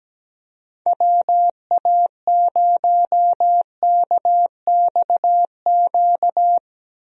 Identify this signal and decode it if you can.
Morse code, heard as a keyed tone, about 17 wpm, 700 Hz. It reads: WA0KXQ
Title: USA Call Signs - 6 Characters - 17wpm